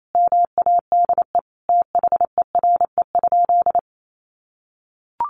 Morse code, heard as a keyed tone, about 28 wpm, 700 Hz. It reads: MADE THERE?
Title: Sentences from Top 1400 Words 28wpm